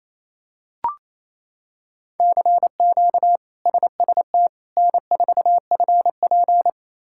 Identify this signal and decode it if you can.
Morse code, heard as a keyed tone, about 28 wpm, 700 Hz. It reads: CQ SST N4FP
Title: K1USN SST QSO Practice - 28wpm